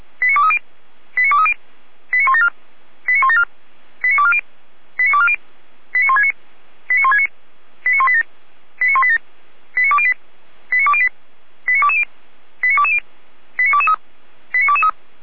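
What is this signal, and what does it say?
Five-tone selective call (ZVEI), heard as quick runs of five tones, tones of about 70 ms. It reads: 89230 89230 89174 89174 89240 89240 89169 89169 89187 89187 89208 89208 89220 89220 89223 89223
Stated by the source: ZVEI five-tone selective call 89167, received by SDR on the 455 channel, fire brigade Feuerwehr Orlamünde.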